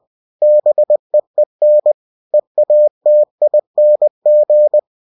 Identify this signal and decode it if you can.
Morse code, heard as a keyed tone, about 20 wpm, 600 Hz. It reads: BEEN EATING
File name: cw-books / winnie-the-pooh / winnie-20wpm-page-0040